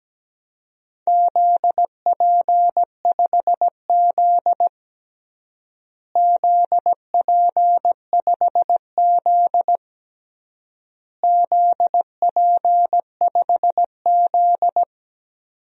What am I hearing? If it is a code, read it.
Morse code, heard as a keyed tone, about 17 wpm, 700 Hz. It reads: ZP5Z ZP5Z ZP5Z
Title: DXCC-100 Call Signs - 4 Characters - Repeated 3x - 17wpm